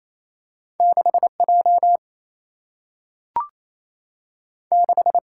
Morse code, heard as a keyed tone, about 28 wpm, 700 Hz. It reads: 6 J 6
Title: Single-Letter Single-Number 28wpm